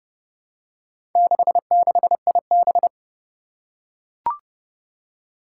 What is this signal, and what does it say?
Morse code, heard as a keyed tone, about 30 wpm, 700 Hz. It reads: 66IB
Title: Four Challenging Letter-Number Combinations 30wpm